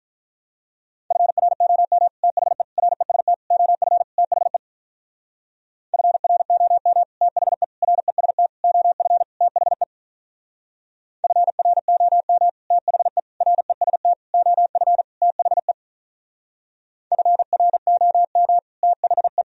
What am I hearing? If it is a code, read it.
Morse code, heard as a keyed tone, about 44 wpm, 700 Hz. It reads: FROM THE REST OF THE FROM THE REST OF THE FROM THE REST OF THE FROM THE